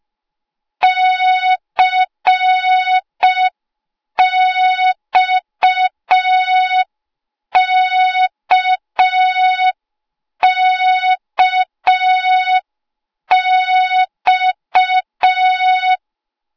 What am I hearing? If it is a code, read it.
Morse code, heard as a keyed tone, about 5 wpm, 3700 Hz. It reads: CXKKX
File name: Morze-HA2MN / hang / 08KXC HA2MN